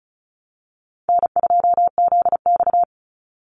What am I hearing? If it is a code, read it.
Morse code, heard as a keyed tone, about 34 wpm, 700 Hz. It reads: N2ZX
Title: Easy Call Signs Repeated 3x - 35wpm